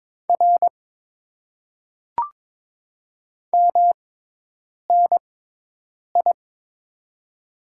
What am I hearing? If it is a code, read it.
Morse code, heard as a keyed tone, about 22 wpm, spaced as for 8 wpm, 700 Hz. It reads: R MNI
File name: Top 90 QSO Elements - 6x Character Spacing - 22wpm